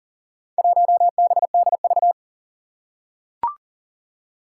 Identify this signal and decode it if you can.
Morse code, heard as a keyed tone, about 40 wpm, 700 Hz. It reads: 1BDV
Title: Four Challenging Letter-Number Combinations 40wpm